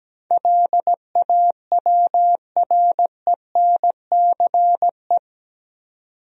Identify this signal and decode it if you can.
Morse code, heard as a keyed tone, about 17 wpm, 700 Hz. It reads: LAWRENCE
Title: USA Common Names - 17wpm